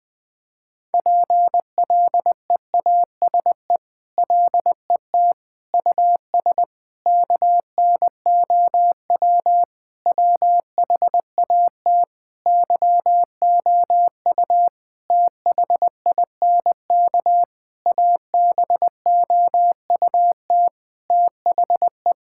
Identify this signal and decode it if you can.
Morse code, heard as a keyed tone, about 20 wpm, 700 Hz. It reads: PLEASE LET US KNOW WHAT YOU THINK ABOUT THE
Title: Sentences with 6-Word Phrases - Spoken Beginning - 20wpm